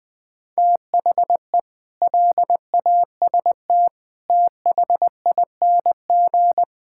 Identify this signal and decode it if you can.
Morse code, heard as a keyed tone, about 21 wpm, 700 Hz. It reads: THE LAST THING
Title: Sets of 8 Words - 20wpm